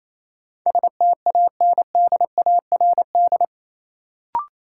Morse code, heard as a keyed tone, about 28 wpm, 700 Hz. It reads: STANDARD
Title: Sentences from Top 1200 Words - New Word Drill Alt - 28wpm